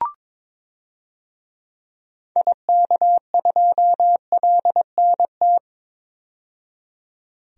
Morse code, heard as a keyed tone, about 22 wpm, 700 Hz. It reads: IK2LNT
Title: DXCC-100 Call Signs - 6 Characters - 22wpm